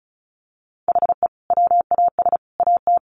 Morse code, heard as a keyed tone, about 34 wpm, 700 Hz. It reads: HE WAS AT
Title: Sentences from Top 1400 Words 35wpm